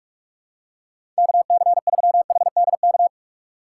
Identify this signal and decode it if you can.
Morse code, heard as a keyed tone, about 46 wpm, 700 Hz. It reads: KC3HDK
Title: USA Call Signs - 6 Characters - 4x Character Spacing - 45wpm